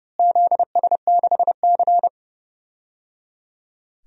Morse code, heard as a keyed tone, about 30 wpm, 700 Hz. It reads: ZS6C
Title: DXCC-100 Call Signs - 4 Characters - Repeated 3x - 30wpm